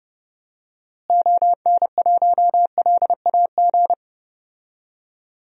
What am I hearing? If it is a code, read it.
Morse code, heard as a keyed tone, about 32 wpm, 700 Hz. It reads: ON1LAG
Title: DXCC-100 Call Signs - 6 Characters - Repeated 3x - 30wpm